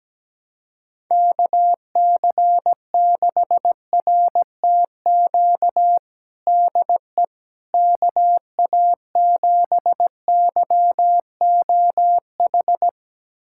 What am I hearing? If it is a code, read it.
Morse code, heard as a keyed tone, about 17 wpm, 700 Hz. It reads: KC6RTQ DE KA7YOH
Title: USA Callsign DE Callsign - 6 Characters - 17wpm